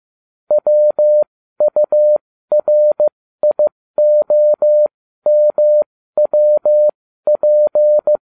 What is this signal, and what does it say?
Morse code, heard as a keyed tone, about 15 wpm, spaced as for 13 wpm, 600 Hz. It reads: WURIOMWP